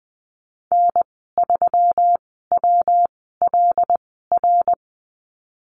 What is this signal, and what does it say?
Morse code, heard as a keyed tone, about 20 wpm, spaced as for 15 wpm, 700 Hz. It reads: N3WLR
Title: 173 Lesson 28g - Number 9 - Call Signs 20wpm